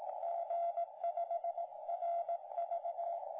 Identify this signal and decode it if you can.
Morse code, heard as a keyed tone, about 18 wpm, 680 Hz, with radio noise. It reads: G5RV